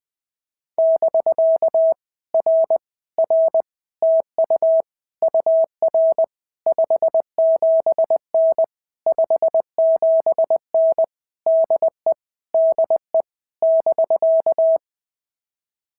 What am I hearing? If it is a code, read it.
Morse code, heard as a keyed tone, about 21 wpm, 650 Hz. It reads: <BK> R R TU UR 57N 57N DE DE <BK>
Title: POTA QSO Practice - 20wpm